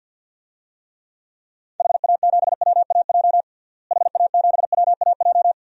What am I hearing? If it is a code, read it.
Morse code, heard as a keyed tone, about 51 wpm, 700 Hz. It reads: HA7WAJ HA7WAJ
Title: DXCC-100 Call Signs - 6 Characters - Repeated 2x - 50wpm